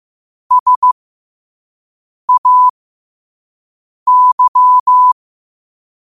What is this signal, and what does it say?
Morse code, heard as a keyed tone, about 15 wpm, spaced as for 5 wpm, 1000 Hz. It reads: SAY
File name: qotdfiles / 5wpm